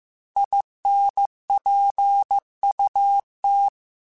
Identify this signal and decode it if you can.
Morse code, heard as a keyed tone, about 15 wpm, 800 Hz. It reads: INPUT
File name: Technical Subjects / Technical Subjects-mp3s / OR Gate @13